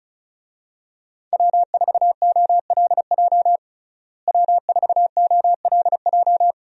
Morse code, heard as a keyed tone, about 36 wpm, 700 Hz. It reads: W4OLJ W4OLJ
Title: USA Call Signs - 5 Characters - Repeated 2x - 35wpm